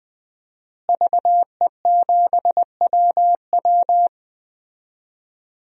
Morse code, heard as a keyed tone, about 20 wpm, 700 Hz. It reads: VE7WW
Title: DXCC-100 Call Signs - 5 Characters - Repeated 3x - 20wpm